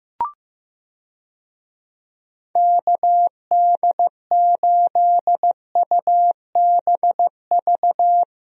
Morse code, heard as a keyed tone, about 15 wpm, 700 Hz. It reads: KD8UBV
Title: USA Call Signs - 6 Characters - 15wpm